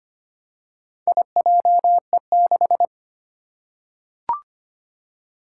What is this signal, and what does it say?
Morse code, heard as a keyed tone, about 25 wpm, 700 Hz. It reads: IJE6
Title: Four Challenging Letter-Number Combinations 25wpm